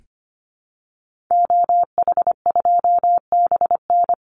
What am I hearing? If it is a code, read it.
Morse code, heard as a keyed tone, about 25 wpm, 700 Hz. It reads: OH2BN
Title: Call Signs Repeated 2x - 25wpm